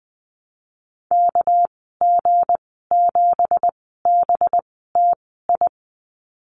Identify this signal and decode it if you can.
Morse code, heard as a keyed tone, about 20 wpm, 700 Hz. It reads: K G 7 B T I
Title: 231 Lesson 37g - Letter x - Call Signs 20wpm